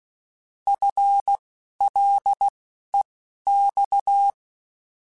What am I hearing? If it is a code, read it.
Morse code, heard as a keyed tone, about 16 wpm, spaced as for 12 wpm, 800 Hz. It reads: FLEX
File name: FL / FL mp3s / Fast FL @10